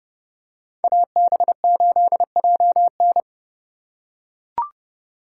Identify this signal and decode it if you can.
Morse code, heard as a keyed tone, about 30 wpm, 700 Hz. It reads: AB8JN